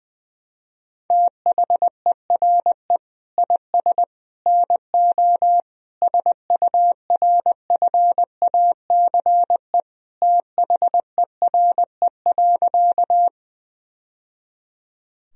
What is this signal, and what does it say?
Morse code, heard as a keyed tone, about 21 wpm, 700 Hz. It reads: THERE IS NO SURFACE THERE.
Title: Sentences from Top 1000 Words - Review Part 3 - 20wpm